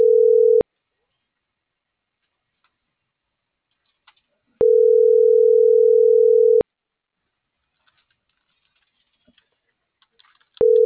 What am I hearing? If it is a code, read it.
Ringback tone (North America), 440 + 480 Hz.